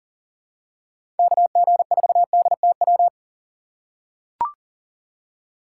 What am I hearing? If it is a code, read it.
Morse code, heard as a keyed tone, about 40 wpm, 700 Hz. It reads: KG4DTW
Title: USA Call Signs - 6 Characters - 4x Character Spacing - 40wpm